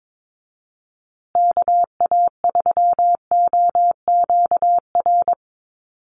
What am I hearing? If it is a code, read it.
Morse code, heard as a keyed tone, about 22 wpm, 700 Hz. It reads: KA3OQR
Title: USA Call Signs Repeated 3x 22wpm